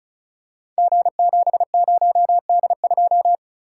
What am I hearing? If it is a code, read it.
Morse code, heard as a keyed tone, about 36 wpm, 700 Hz. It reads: G70D2
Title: Code Groups - 5 Letter:Numbers - 35wpm